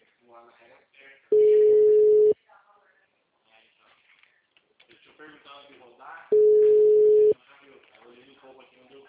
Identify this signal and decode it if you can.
Ringback tone (Europe, 425 Hz), with speech.